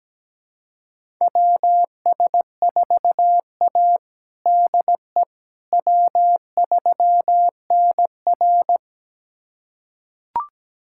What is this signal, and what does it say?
Morse code, heard as a keyed tone, about 17 wpm, 700 Hz. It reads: WS4A DE W3NR
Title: USA Callsign DE Callsign - 4 Characters - 17wpm